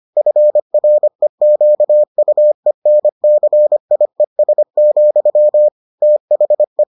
Morse code, heard as a keyed tone, about 25 wpm, 600 Hz. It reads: FREQUENCIES, THE